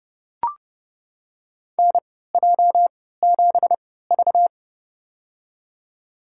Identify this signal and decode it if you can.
Morse code, heard as a keyed tone, about 32 wpm, spaced as for 17 wpm, 700 Hz. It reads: NJ7V
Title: USA Call Signs - 4 Characters - 3x Character Spacing - 30wpm